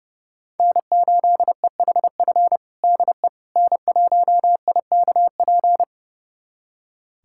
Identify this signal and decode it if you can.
Morse code, heard as a keyed tone, about 30 wpm, 700 Hz. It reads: N8EHF DE N1IKP
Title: USA Callsign DE Callsign - 5 Characters - 30wpm